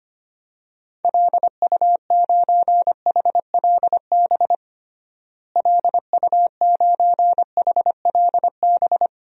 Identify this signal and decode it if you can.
Morse code, heard as a keyed tone, about 25 wpm, 700 Hz. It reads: LU9HLB LU9HLB
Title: DXCC-100 Call Signs - 6 Characters - Repeated 2x - 25wpm